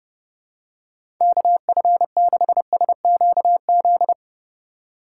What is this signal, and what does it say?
Morse code, heard as a keyed tone, about 30 wpm, 700 Hz. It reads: KF6SQZ
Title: USA Call Signs - 6 Characters - 4x Character Spacing - 30wpm